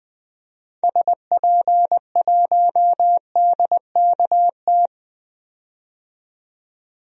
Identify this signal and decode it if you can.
Morse code, heard as a keyed tone, about 20 wpm, 700 Hz. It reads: SP1DKT
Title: DXCC-100 Call Signs - 6 Characters - 20wpm